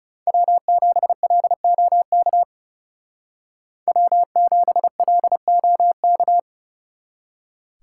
Morse code, heard as a keyed tone, about 37 wpm, 700 Hz. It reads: W7LOK W7LOK
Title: Speed Racing - USA 1x3 Call Signs 30wpm